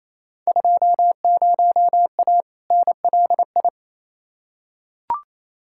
Morse code, heard as a keyed tone, about 28 wpm, 700 Hz. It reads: 20A NLI
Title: Field Day Exchanges Encore - 28wpm